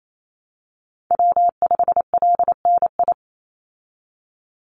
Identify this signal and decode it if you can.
Morse code, heard as a keyed tone, about 28 wpm, 700 Hz. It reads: W5LNI
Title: Call Signs - 28wpm